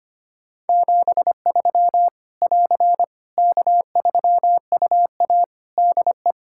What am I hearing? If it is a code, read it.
Morse code, heard as a keyed tone, about 25 wpm, 700 Hz. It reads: 73 <AR> K3UA DE
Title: Practice QSO - Exchange 4 - Ending - 25wpm